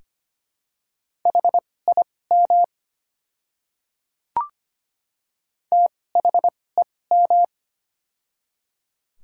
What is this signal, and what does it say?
Morse code, heard as a keyed tone, about 25 wpm, spaced as for 18 wpm, 700 Hz. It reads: HIM THEM
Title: 2x Character Spacing - Top 200 Words - 25wpm